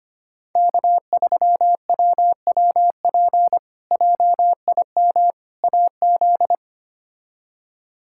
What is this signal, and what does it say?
Morse code, heard as a keyed tone, about 25 wpm, 700 Hz. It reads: K3WWP JIM AZ
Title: K1USN SST QSO Practice - 25wpm